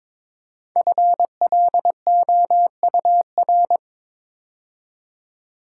Morse code, heard as a keyed tone, about 22 wpm, 700 Hz. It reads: FLOUR
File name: 5 Letter Words - First 3 Letters Given - 22wpm